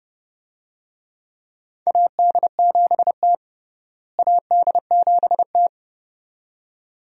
Morse code, heard as a keyed tone, about 32 wpm, 700 Hz. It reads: AD7T AD7T
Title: USA Call Signs - 4 Characters - Repeated 2x - 30wpm